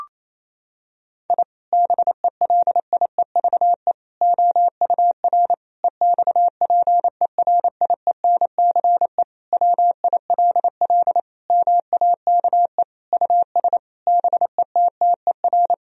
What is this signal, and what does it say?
Morse code, heard as a keyed tone, about 27 wpm, 700 Hz. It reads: I BELIEVE OUR EXPERIENCE WILL MAKE US BETTER
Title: Sentences from Top 1000 Words - Review Part 3 - 28wpm